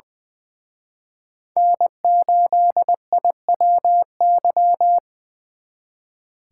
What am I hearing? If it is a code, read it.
Morse code, heard as a keyed tone, about 24 wpm, 700 Hz. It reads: N8IWY
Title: Speed Racing - USA 1x3 Call Signs 20wpm